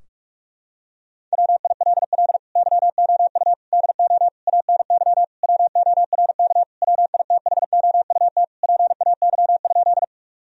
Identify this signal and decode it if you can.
Morse code, heard as a keyed tone, about 46 wpm, 700 Hz. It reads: WILL YOU DO ANY WORK WITHOUT PAY?
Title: Sentences from Top 1000 Words - Review Part 3 - 45wpm